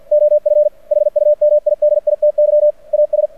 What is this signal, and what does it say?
Morse code, heard as a keyed tone, about 51 wpm, 600 Hz. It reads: OF HUMIDITY AS
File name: archived / CRSnet